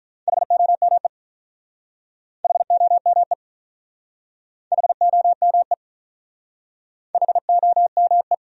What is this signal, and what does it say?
Morse code, heard as a keyed tone, about 47 wpm, 700 Hz. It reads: HOME HOME HOME HOME